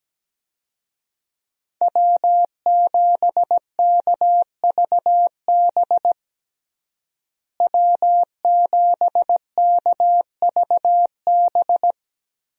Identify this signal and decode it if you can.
Morse code, heard as a keyed tone, about 17 wpm, 700 Hz. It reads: W7KVB W7KVB
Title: USA Call Signs - 5 Characters - Repeated 3x - 17wpm